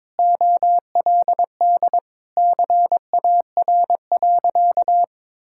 Morse code, heard as a keyed tone, about 22 wpm, 700 Hz. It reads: OLD CAR.
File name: practice / 22wpm / Sentences with 8-Word Phrases - 22wpm